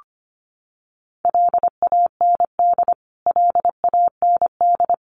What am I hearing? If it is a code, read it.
Morse code, heard as a keyed tone, about 25 wpm, 700 Hz. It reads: LAND LAND
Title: Top 500 Words Repeated 25wpm